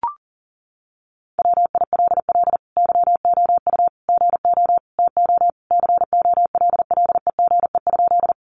Morse code, heard as a keyed tone, about 40 wpm, 700 Hz. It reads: WILL YOU GO TO COLLEGE?
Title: Sentences from Top 1300 Words 40wpm